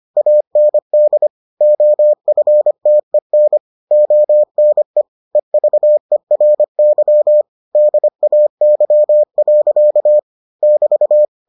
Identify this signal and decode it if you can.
Morse code, heard as a keyed tone, about 25 wpm, 600 Hz. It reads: AND OFTEN ONE EVERY DAY. <BT>